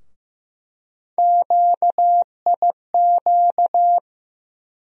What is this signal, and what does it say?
Morse code, heard as a keyed tone, about 15 wpm, 700 Hz. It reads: QIQ
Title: USA Call Sign Postfixes - 3 Letters 15wpm